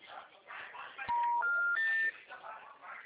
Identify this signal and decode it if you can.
Special information tone, with speech.
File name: out- -2007- - - .3553